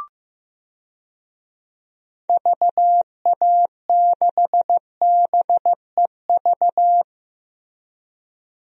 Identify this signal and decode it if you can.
Morse code, heard as a keyed tone, about 15 wpm, 700 Hz. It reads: VA6BEV
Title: DXCC-100 Call Signs - 6 Characters - Repeated 3x - 15wpm